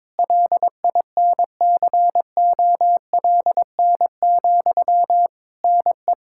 Morse code, heard as a keyed tone, about 21 wpm, 700 Hz. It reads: LINCOLN, NE
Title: Practice QSO - Exchange 1 - Beginning - 22wpm